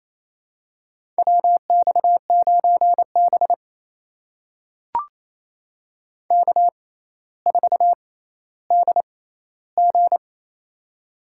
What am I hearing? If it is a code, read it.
Morse code, heard as a keyed tone, about 28 wpm, spaced as for 14 wpm, 700 Hz. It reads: WX9B K4DG
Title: USA Call Signs - 4 Characters - 6x Character Spacing - 28wpm